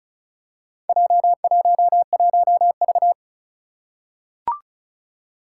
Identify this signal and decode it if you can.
Morse code, heard as a keyed tone, about 36 wpm, 700 Hz. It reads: J11V
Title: Four Challenging Letter-Number Combinations 35wpm